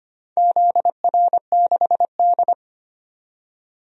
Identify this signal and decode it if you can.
Morse code, heard as a keyed tone, about 25 wpm, 700 Hz. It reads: ZR6D
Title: DXCC-100 Call Signs - 4 Characters - 25wpm